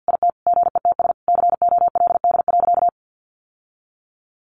Morse code, heard as a keyed tone, about 50 wpm, 700 Hz. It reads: IT GETS COLD.
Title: Sentences from Top 1300 Words 50wpm